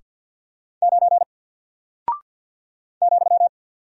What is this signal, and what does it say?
Morse code, heard as a keyed tone, about 48 wpm, 700 Hz. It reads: 9 ,
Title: Single Character (Letter:Number:Punctuation:Prosign) 50wpm